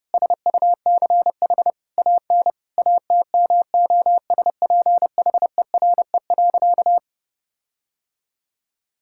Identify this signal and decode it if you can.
Morse code, heard as a keyed tone, about 30 wpm, 700 Hz. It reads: SUCH AN ATMOSPHERE.
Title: Sentences from Top 1600 Words 30wpm